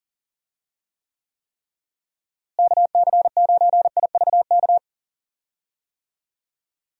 Morse code, heard as a keyed tone, about 40 wpm, 700 Hz. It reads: KC9IVK
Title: USA Call Signs - 6 Characters - Repeated 3x - 40wpm